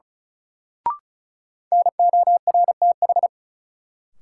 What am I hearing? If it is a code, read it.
Morse code, heard as a keyed tone, about 36 wpm, 700 Hz. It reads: NORTH
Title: Sentences from Top 400 Words - New Word Drill - 35wpm